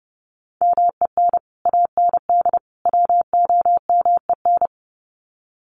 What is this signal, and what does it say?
Morse code, heard as a keyed tone, about 30 wpm, 700 Hz. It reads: MEN AND WOMEN
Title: Sets of 3 Words 30wpm